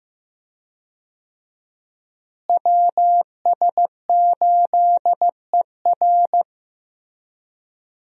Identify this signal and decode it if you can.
Morse code, heard as a keyed tone, about 15 wpm, 700 Hz. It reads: WS8ER
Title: USA Call Signs - 5 Characters - Repeated 3x - 15wpm